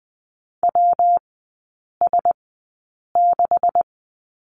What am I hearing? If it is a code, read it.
Morse code, heard as a keyed tone, about 20 wpm, spaced as for 8 wpm, 700 Hz. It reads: WS6
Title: 172 Lesson 28e - Number 9 - Three Characters 20wpm